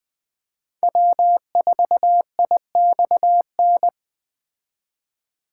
Morse code, heard as a keyed tone, about 19 wpm, 700 Hz. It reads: W4IXN